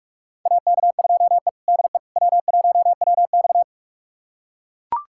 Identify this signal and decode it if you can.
Morse code, heard as a keyed tone, about 42 wpm, 700 Hz. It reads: AK2E DE W1WX
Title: USA Callsign DE Callsign - 4 Characters - 45wpm